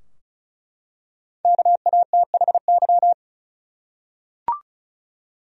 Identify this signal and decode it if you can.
Morse code, heard as a keyed tone, about 36 wpm, 700 Hz. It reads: KATHY
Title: USA Common Names - Max 5 Letters - 35wpm